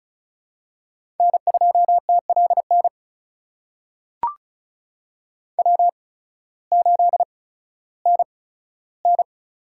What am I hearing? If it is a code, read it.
Morse code, heard as a keyed tone, about 36 wpm, spaced as for 16 wpm, 700 Hz. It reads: N2TLN W8NN
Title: USA Call Signs - 5 Characters - 8x Character Spacing - 35wpm